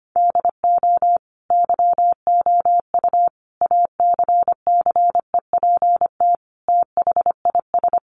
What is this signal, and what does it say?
Morse code, heard as a keyed tone, about 25 wpm, 700 Hz. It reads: DO YOU ACCEPT THIS
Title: Sentences from Top 1300 Words 25wpm